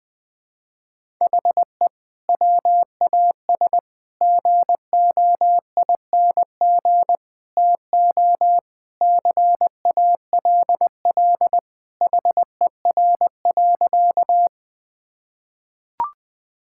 Morse code, heard as a keyed tone, about 20 wpm, 700 Hz. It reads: HE WAS GOING TO CALL HER.
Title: Sentences with 4-Word Phrases - 20wpm